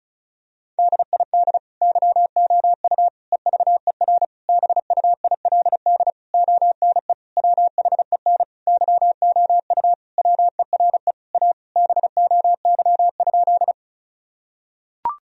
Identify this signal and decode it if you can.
Morse code, heard as a keyed tone, about 36 wpm, 700 Hz. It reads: DID YOU EVER BUILD ONE WHEN YOU WERE A BOY?
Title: Sentences from Top 1000 Words - Review Part 1 - 35wpm